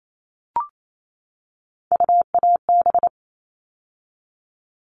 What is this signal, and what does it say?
Morse code, heard as a keyed tone, about 28 wpm, 700 Hz. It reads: UAB
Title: All Cut Numbers - 3-Digit 28wpm